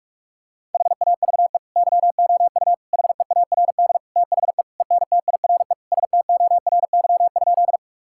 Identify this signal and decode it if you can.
Morse code, heard as a keyed tone, about 46 wpm, 700 Hz. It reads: HAVE YOU HEARD THE ENTIRE STORY?
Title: Sentences from Top 1000 Words - Review Part 5 - 45wpm